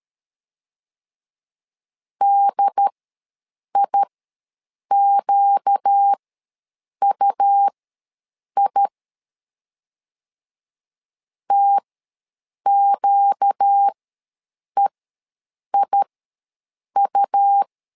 Morse code, heard as a keyed tone, about 13 wpm, spaced as for 7 wpm, 800 Hz. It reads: DIQUI TQEIU
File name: LEZIONI / 4 Q ws 12wpm 1912